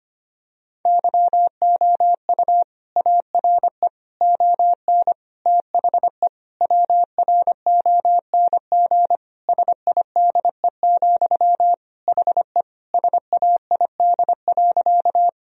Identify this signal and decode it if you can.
Morse code, heard as a keyed tone, about 25 wpm, 700 Hz. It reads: YOU ARE ON THE WRONG SIDE, HE SAID.
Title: Sentences from Top 1000 Words - Review Part 1 - 25wpm